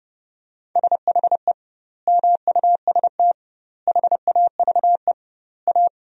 Morse code, heard as a keyed tone, about 30 wpm, 700 Hz. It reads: SHE MUST HAVE A
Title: 2x Word-Spacing - Sentences from Top 300 Words - 30wpm